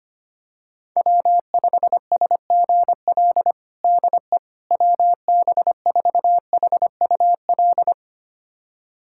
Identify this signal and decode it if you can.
Morse code, heard as a keyed tone, about 25 wpm, 700 Hz. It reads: W5SGL DE WB4HUL
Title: USA Callsign DE Callsign 25wpm